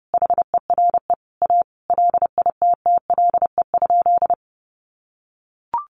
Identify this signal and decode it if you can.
Morse code, heard as a keyed tone, about 30 wpm, 700 Hz. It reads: HERE A LITTLE?
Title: Sentences from Top 500 Words 30wpm